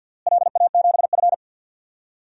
Morse code, heard as a keyed tone, about 50 wpm, 700 Hz. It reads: LA7F
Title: DXCC-100 Call Signs - 4 Characters - Repeated 3x - 50wpm